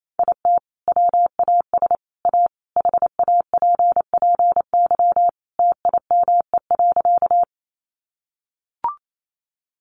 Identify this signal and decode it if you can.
Morse code, heard as a keyed tone, about 28 wpm, 700 Hz. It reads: IT WAS A HAPPY TIME.
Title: Sentences from Top 1000 Words 28wpm